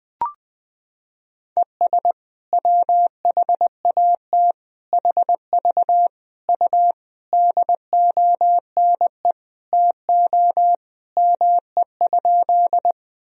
Morse code, heard as a keyed tone, about 21 wpm, 700 Hz. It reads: ES WHAT HV U DONE TO ME?
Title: Sentences from Top 500 Words and Abbreviations 20wpm